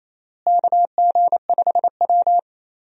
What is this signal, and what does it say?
Morse code, heard as a keyed tone, about 28 wpm, 700 Hz. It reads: KG5W
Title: USA Call Signs - 4 Characters - Repeated 2x - 28wpm